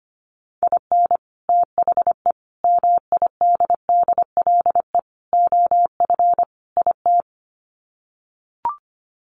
Morse code, heard as a keyed tone, about 25 wpm, 700 Hz. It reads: IN THE MIDDLE OF IT
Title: Sets of 5 Words Encore 25wpm